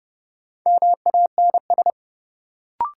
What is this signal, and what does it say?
Morse code, heard as a keyed tone, about 29 wpm, 700 Hz. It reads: MANS
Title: Sentences from Top 1000 Words - Review Part 6 - New Word Drill - 30wpm